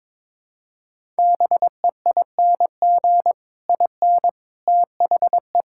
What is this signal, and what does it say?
Morse code, heard as a keyed tone, about 22 wpm, 700 Hz. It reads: BEING IN THE
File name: Sets of 9 Words - 22wpm